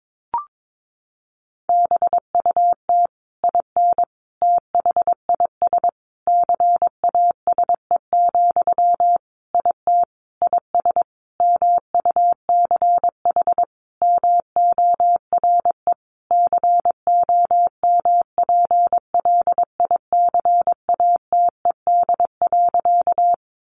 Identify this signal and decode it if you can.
Morse code, heard as a keyed tone, about 22 wpm, 700 Hz. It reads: BUT IN THIS CASE, IT IS MUCH MORE COMPLICATED.
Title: Sentences from Top 1200 Words 22wpm